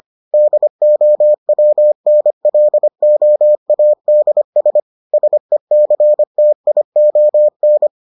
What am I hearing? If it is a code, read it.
Morse code, heard as a keyed tone, about 25 wpm, 600 Hz. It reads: DOWNLOADS SECTION